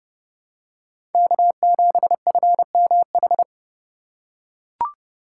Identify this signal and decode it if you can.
Morse code, heard as a keyed tone, about 32 wpm, 700 Hz. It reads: K7FMH
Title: USA Call Signs - 5 Characters - 6x Character Spacing - 30wpm